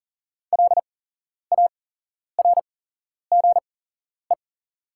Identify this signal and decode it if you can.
Morse code, heard as a keyed tone, about 40 wpm, spaced as for 11 wpm, 700 Hz. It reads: LARGE
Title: 8x Character Spacing - Top 500 Words - 40wpm